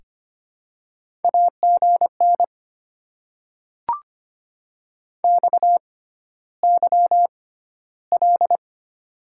Morse code, heard as a keyed tone, about 25 wpm, spaced as for 13 wpm, 700 Hz. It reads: AGN XYL
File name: Top 90 QSO Elements - 6x Character Spacing - 25wpm